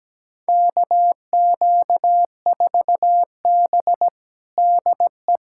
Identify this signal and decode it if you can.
Morse code, heard as a keyed tone, about 17 wpm, 700 Hz. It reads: KQ4B DE
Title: USA Callsign DE Callsign - 4 Characters - 17wpm